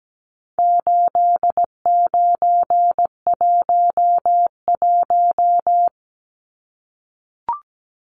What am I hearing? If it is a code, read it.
Morse code, heard as a keyed tone, about 17 wpm, 700 Hz. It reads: 8911
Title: Numbers - 4-Digit 17wpm